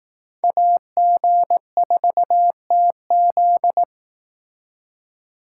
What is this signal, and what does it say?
Morse code, heard as a keyed tone, about 18 wpm, 700 Hz. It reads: AG4TZ
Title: Speed Racing - USA 2x2 Call Signs 15wpm